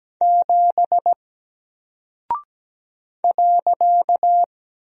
Morse code, heard as a keyed tone, about 17 wpm, 700 Hz. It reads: 7 .
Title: Single Character (Letter:Number:Punctuation:Prosign) Alt 17wpm